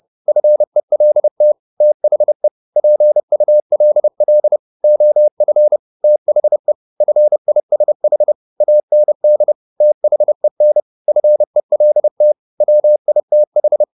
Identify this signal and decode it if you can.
Morse code, heard as a keyed tone, about 30 wpm, 600 Hz. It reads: FELT THE PULL OF THE FISH AND THEN FELT WITH
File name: the old man and the sea / the old man and the sea 1 30wpm